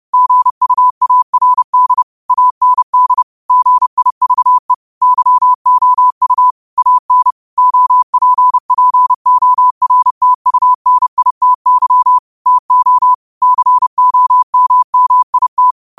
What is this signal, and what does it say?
Morse code, heard as a keyed tone, about 30 wpm, 1000 Hz. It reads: GUARD AND GIVE YOU AN OPPORTUNITY TO COMMIT
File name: qotdfiles / 30wpm